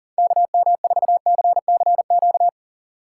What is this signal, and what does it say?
Morse code, heard as a keyed tone, about 51 wpm, 700 Hz. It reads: KM4CCQ